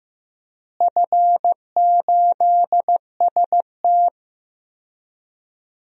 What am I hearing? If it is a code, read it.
Morse code, heard as a keyed tone, about 15 wpm, 700 Hz. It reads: F8ST
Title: DXCC-100 Call Signs - 4 Characters - Repeated 3x - 15wpm